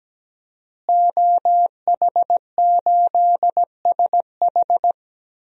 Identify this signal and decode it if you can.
Morse code, heard as a keyed tone, about 17 wpm, 700 Hz. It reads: OH8SH
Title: DXCC-100 Call Signs - 5 Characters - Repeated 3x - 17wpm